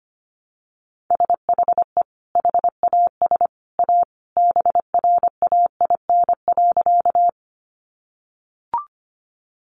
Morse code, heard as a keyed tone, about 25 wpm, 700 Hz. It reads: SHE HAS A BRAIN.
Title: Sentences from Top 1100 Words 25wpm